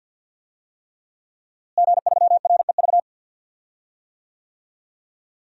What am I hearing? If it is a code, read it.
Morse code, heard as a keyed tone, about 53 wpm, 700 Hz. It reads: G3REV
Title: DXCC-100 Call Signs - 5 Characters - Repeated 3x - 50wpm